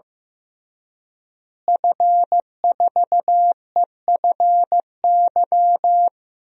Morse code, heard as a keyed tone, about 16 wpm, 700 Hz. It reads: F4EFY
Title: DXCC-100 Call Signs - 5 Characters - Repeated 3x - 15wpm